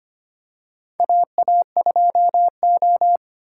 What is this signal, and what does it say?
Morse code, heard as a keyed tone, about 25 wpm, 700 Hz. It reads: AA2O
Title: USA Call Signs - 4 Characters - 4x Character Spacing - 25wpm